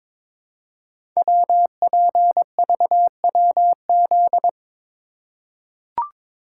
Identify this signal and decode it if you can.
Morse code, heard as a keyed tone, about 22 wpm, 700 Hz. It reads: WPVWZ
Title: Code Groups - 5 Letters - 22wpm